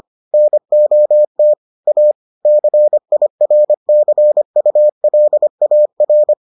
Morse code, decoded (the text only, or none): NOT A CIRCULAR